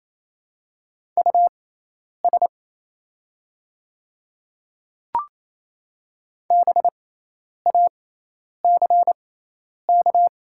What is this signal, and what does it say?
Morse code, heard as a keyed tone, about 28 wpm, spaced as for 10 wpm, 700 Hz. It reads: US BACK
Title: 6x Character Spacing - Top 200 Words - 28wpm